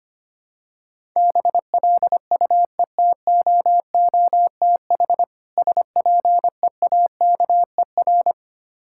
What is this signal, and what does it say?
Morse code, heard as a keyed tone, about 25 wpm, 700 Hz. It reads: BLUETOOTH SPEAKER
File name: practice / 25wpm / Difficult Words with Hints - 25wpm